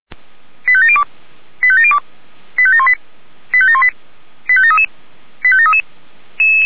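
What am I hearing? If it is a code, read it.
Five-tone selective call (ZVEI), heard as quick runs of five tones, tones of about 70 ms. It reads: 85702 85702 85718 85718 85733 85733
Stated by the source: ZVEI five-tone selective call received on an SDR.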